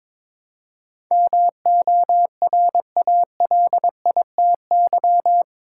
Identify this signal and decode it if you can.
Morse code, heard as a keyed tone, about 22 wpm, 700 Hz. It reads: MORALITY
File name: practice / 22wpm / Eight Letter Words - 22wpm